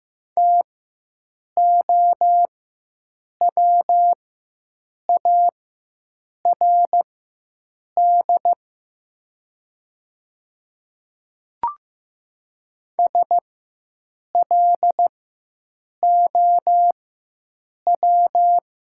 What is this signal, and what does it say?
Morse code, heard as a keyed tone, about 15 wpm, spaced as for 7 wpm, 700 Hz. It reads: TOWARD SLOW
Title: 4x Character Spacing - Top 500 Words - 15wpm